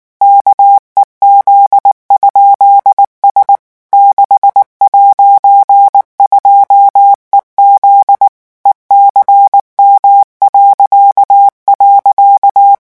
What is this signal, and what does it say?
Morse code, heard as a keyed tone, about 19 wpm, 800 Hz. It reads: KEZ?S 6'2EZ ECM..
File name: morse mp3 / revision15-vitesse 18 mots